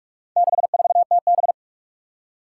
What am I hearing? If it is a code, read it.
Morse code, heard as a keyed tone, about 42 wpm, 700 Hz. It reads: 64TB
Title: Four Challenging Letter-Number Combinations 45wpm